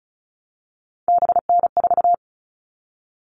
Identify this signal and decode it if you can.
Morse code, heard as a keyed tone, about 34 wpm, 700 Hz. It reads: BN4
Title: All Cut Numbers - 3-Digit 35wpm